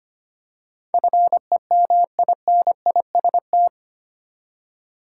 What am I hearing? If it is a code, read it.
Morse code, heard as a keyed tone, about 25 wpm, 700 Hz. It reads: FEMINIST
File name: practice / 25wpm / 8 Letter Words - First 3 Letters Given - 25wpm